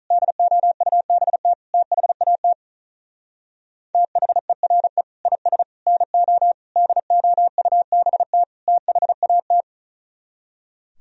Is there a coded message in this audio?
Yes: Morse code from 0.1 to 9.6 s, heard as a keyed tone, about 46 wpm, 700 Hz. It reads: DOUBT THAT THERE IS NO DOUBT THAT